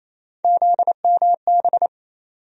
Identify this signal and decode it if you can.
Morse code, heard as a keyed tone, about 28 wpm, 700 Hz. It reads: ZMB
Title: USA Call Sign Postfixes - 3 Letters 28wpm